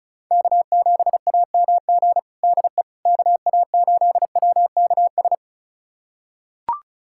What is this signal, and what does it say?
Morse code, heard as a keyed tone, about 34 wpm, 700 Hz. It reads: K7AMG DE KA8WKS
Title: USA Callsign DE Callsign 35wpm